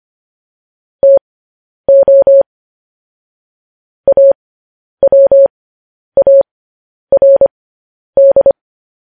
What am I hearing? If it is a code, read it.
Morse code, heard as a keyed tone, about 25 wpm, spaced as for 10 wpm, 550 Hz. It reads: TO AWARD